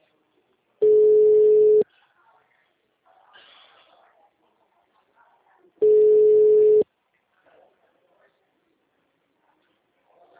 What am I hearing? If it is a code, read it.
Ringback tone (Europe, 425 Hz), with speech.